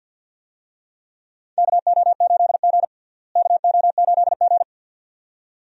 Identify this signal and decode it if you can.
Morse code, heard as a keyed tone, about 50 wpm, 700 Hz. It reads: KO8G KO8G
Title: USA Call Signs - 4 Characters - Repeated 2x - 50wpm